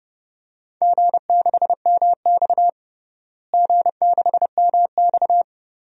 Morse code, heard as a keyed tone, about 32 wpm, 700 Hz. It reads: G6MX G6MX
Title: DXCC-100 Call Signs - 4 Characters - Repeated 3x - 30wpm